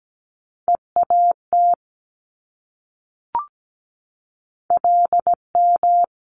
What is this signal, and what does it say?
Morse code, heard as a keyed tone, about 17 wpm, 690 Hz. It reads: EAT LM
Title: Multi-letter Phonemes Repeated 3x 17wpm